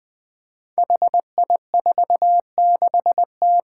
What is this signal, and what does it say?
Morse code, heard as a keyed tone, about 20 wpm, 700 Hz. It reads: HI46T
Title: Five Challenging Letter-Number Combinations 20wpm